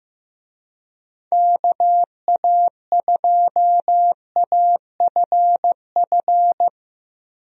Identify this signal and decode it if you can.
Morse code, heard as a keyed tone, about 15 wpm, 700 Hz. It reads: KA2AFF
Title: USA Call Signs - 6 Characters - Repeated 3x - 15wpm